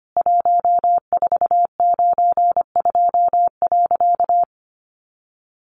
Morse code, heard as a keyed tone, about 25 wpm, 700 Hz. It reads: 1492.
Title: A Sound Of Thunder 25wpm